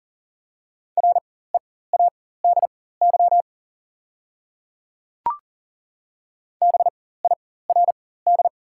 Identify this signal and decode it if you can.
Morse code, heard as a keyed tone, about 40 wpm, spaced as for 19 wpm, 700 Hz. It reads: READY BIRD
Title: 4x Character Spacing - Top 500 Words - 40wpm